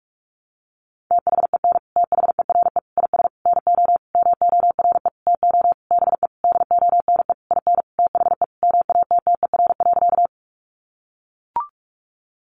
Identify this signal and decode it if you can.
Morse code, heard as a keyed tone, about 44 wpm, 700 Hz. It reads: THEN THERE IS NO MORE TO BE DONE IN THE MATTER.
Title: Sentences from Top 700 Words 45wpm